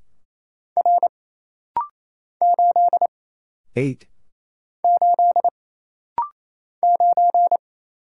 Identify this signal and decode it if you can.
Morse code, heard as a keyed tone, about 27 wpm, 700 Hz. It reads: R 8 8 9
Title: Single Letter-Number - Rapid-Fire 28wpm